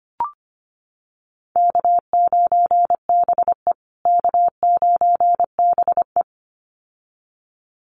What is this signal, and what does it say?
Morse code, heard as a keyed tone, about 25 wpm, 700 Hz. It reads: K9BE K9BE
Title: USA Call Signs Repeated 2x 25wpm